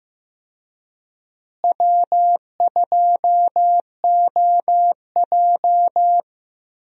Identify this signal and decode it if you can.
Morse code, heard as a keyed tone, about 15 wpm, 700 Hz. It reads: W2OJ
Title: USA Call Signs - 4 Characters - Repeated 3x - 15wpm